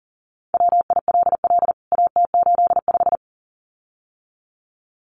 Morse code, heard as a keyed tone, about 40 wpm, 700 Hz. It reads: WILL AT85
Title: CWT Exchange 40wpm